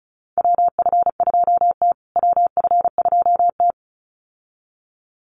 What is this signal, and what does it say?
Morse code, heard as a keyed tone, about 34 wpm, 700 Hz. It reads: WF2T WF2T
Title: USA Call Signs Repeated 2x 35wpm